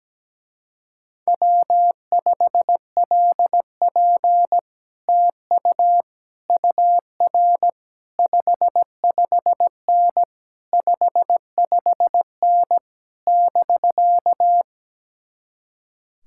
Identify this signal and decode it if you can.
Morse code, heard as a keyed tone, about 17 wpm, 700 Hz. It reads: W5LP TU UR 55N 55N <BK>
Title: Speed-Racing Common QSO Phrases - 17wpm